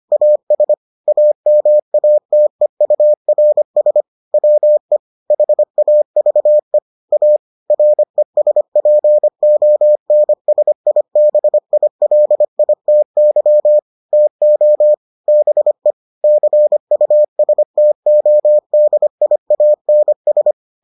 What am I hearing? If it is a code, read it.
Morse code, heard as a keyed tone, about 25 wpm, 600 Hz. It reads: AS AMATEURS WE HAVE A RESPONSIBILITY TO BE CUSTODIANS